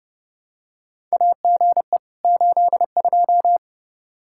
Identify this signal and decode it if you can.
Morse code, heard as a keyed tone, about 30 wpm, 700 Hz. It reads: AGE 82
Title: K1USN SST QSO Practice 30wpm